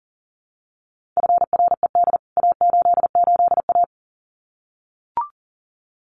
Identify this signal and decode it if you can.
Morse code, heard as a keyed tone, about 40 wpm, 700 Hz. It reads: FRED A88A
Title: CWT Exchange 40wpm